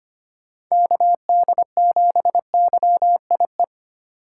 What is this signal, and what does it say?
Morse code, heard as a keyed tone, about 25 wpm, 700 Hz. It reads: KD7YIE
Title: USA Call Signs - 6 Characters - 3x Character Spacing - 25wpm